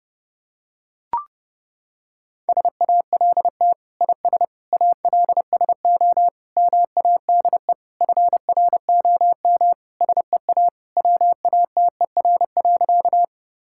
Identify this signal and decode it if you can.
Morse code, heard as a keyed tone, about 32 wpm, 700 Hz. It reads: SALT IS ALSO MADE FROM SEA WATER.
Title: Sentences from Top 1000 Words - Review Part 5 - 30wpm